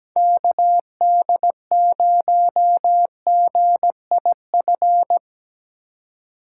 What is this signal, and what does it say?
Morse code, heard as a keyed tone, about 17 wpm, 700 Hz. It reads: KD0GIF
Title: USA Call Signs - 6 Characters - 2x Character Spacing - 17wpm